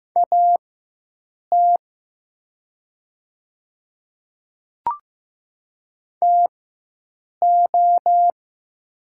Morse code, heard as a keyed tone, about 15 wpm, spaced as for 7 wpm, 700 Hz. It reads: AT TO
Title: 4x Character Spacing - Top 200 Words - 15wpm